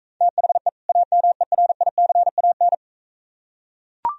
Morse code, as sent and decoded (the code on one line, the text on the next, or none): - .... / .- -- . .-. .. -.-. .- -.
THE AMERICAN